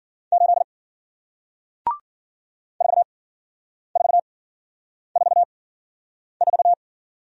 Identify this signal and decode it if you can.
Morse code, heard as a keyed tone, about 54 wpm, spaced as for 9 wpm, 700 Hz. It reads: 8 4444